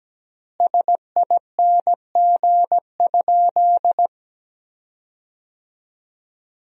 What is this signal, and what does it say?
Morse code, heard as a keyed tone, about 17 wpm, 700 Hz. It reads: SING?
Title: 3x Word-Spacing - Sentences from Top 400 Words - 17wpm